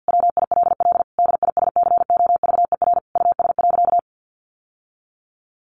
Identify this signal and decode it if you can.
Morse code, heard as a keyed tone, about 48 wpm, 700 Hz. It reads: WILL DISCOVER US.
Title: Sentences from Top 1100 Words 50wpm